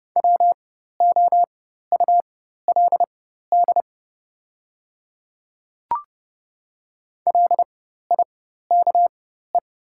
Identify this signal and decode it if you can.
Morse code, heard as a keyed tone, about 30 wpm, spaced as for 14 wpm, 700 Hz. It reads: WOULD LIKE
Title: 4x Character Spacing - Top 100 Words - 30wpm